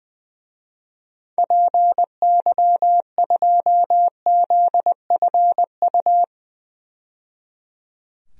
Morse code, heard as a keyed tone, about 20 wpm, 700 Hz. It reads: PY2ZFU
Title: DXCC-100 Call Signs - 6 Characters - 20wpm